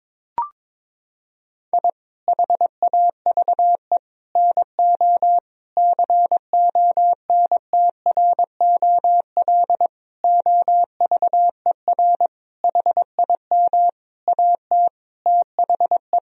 Morse code, heard as a keyed tone, about 22 wpm, 700 Hz. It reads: I HAVE NO CONTROL OVER HIM AT THE
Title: Sentences from Top 1000 Words - Review Part 3 - 22wpm